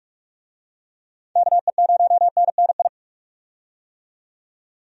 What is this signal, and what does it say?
Morse code, heard as a keyed tone, about 45 wpm, 700 Hz. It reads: KE0NNI
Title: USA Call Signs - 6 Characters - 45wpm